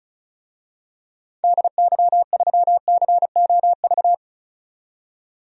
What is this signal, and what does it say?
Morse code, heard as a keyed tone, about 34 wpm, 700 Hz. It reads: DY3COV
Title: DXCC-100 Call Signs - 6 Characters - 35wpm